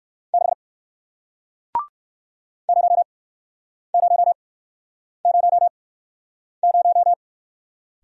Morse code, heard as a keyed tone, about 60 wpm, spaced as for 9 wpm, 700 Hz. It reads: 6 0000